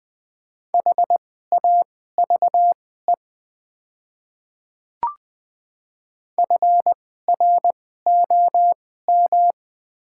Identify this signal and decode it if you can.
Morse code, heard as a keyed tone, about 20 wpm, spaced as for 14 wpm, 700 Hz. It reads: HAVE FROM
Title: 2x Character Spacing - Top 100 Words - 20wpm